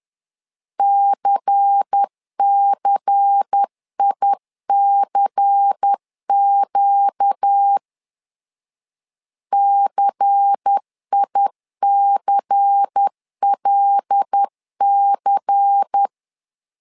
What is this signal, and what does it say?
Morse code, heard as a keyed tone, about 10 wpm, 800 Hz. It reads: CCICQ CICLC
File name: LEZIONI / 5 LBVC / 5 LBVC 10wpm 2349